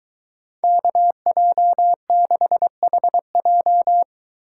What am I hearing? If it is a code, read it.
Morse code, heard as a keyed tone, about 22 wpm, 700 Hz. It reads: KJ6HJ